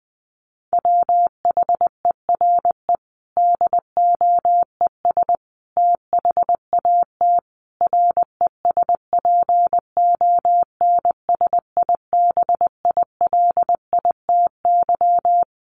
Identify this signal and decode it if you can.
Morse code, heard as a keyed tone, about 20 wpm, 700 Hz. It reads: WHERE DOES THAT RESPONSIBILITY
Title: Sentences from Top 1200 Words 20wpm